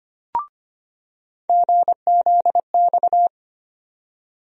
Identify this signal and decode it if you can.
Morse code, heard as a keyed tone, about 25 wpm, 700 Hz. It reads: GZX
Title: USA Call Sign Postfixes - 3 Letters 25wpm